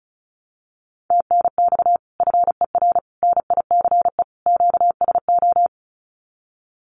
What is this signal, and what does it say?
Morse code, heard as a keyed tone, about 36 wpm, 700 Hz. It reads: TNX FER NICE QSO
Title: Common QSO Phrases Encore 35wpm